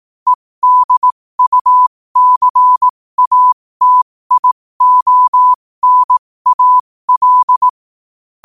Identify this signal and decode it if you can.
Morse code, heard as a keyed tone, about 18 wpm, spaced as for 15 wpm, 1000 Hz. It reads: EDUCATIONAL
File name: 15wpm